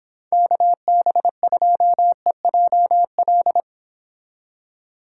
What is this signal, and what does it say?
Morse code, heard as a keyed tone, about 26 wpm, 700 Hz. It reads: KB2EJL